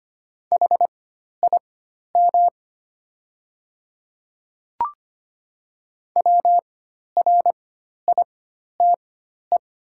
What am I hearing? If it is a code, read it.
Morse code, heard as a keyed tone, about 25 wpm, spaced as for 12 wpm, 700 Hz. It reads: HIM WRITE
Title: 4x Character Spacing - Top 100 Words - 25wpm